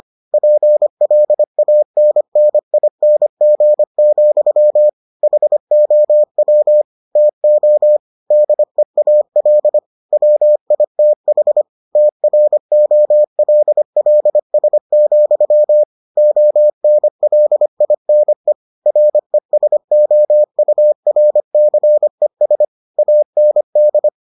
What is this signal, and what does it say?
Morse code, heard as a keyed tone, about 25 wpm, 600 Hz. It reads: PLANNING, HOW TO DEAL WITH TROLLS, ONLINE RESOURCES AND